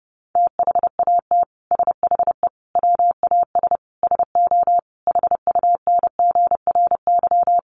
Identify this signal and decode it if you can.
Morse code, heard as a keyed tone, about 30 wpm, 700 Hz. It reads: THAT SHE WAS SO HUNGRY
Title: Sentences from Top 1400 Words 30wpm